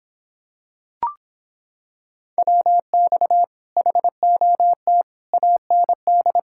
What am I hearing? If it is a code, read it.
Morse code, heard as a keyed tone, about 26 wpm, 700 Hz. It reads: WX HOT AND